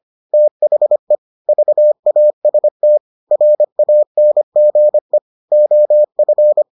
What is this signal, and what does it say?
Morse code, heard as a keyed tone, about 25 wpm, 600 Hz. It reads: THE VAST RANGE OF